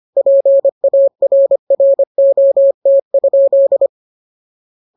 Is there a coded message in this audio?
Yes: Morse code from 0.2 to 3.9 s, heard as a keyed tone, about 25 wpm, 550 Hz. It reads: PARROT?